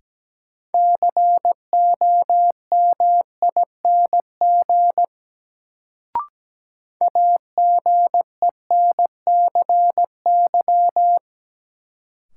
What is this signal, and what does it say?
Morse code, heard as a keyed tone, about 17 wpm, 700 Hz. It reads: COMING AGENCY
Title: Sentences from Top 1300 Words - New Word Drill Alt - 17wpm